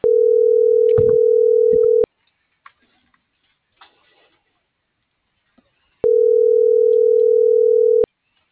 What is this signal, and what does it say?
Ringback tone (North America), 440 + 480 Hz.